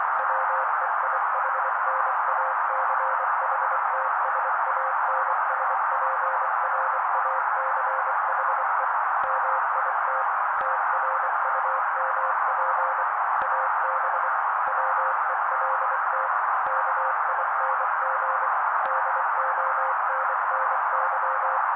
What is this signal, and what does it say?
Morse code, heard as a keyed tone, about 24 wpm, 530 Hz. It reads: WEIHNACHTSANSPRACHE MIT TRUMP AB WELT KING DONNY
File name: daily / 23wpm noise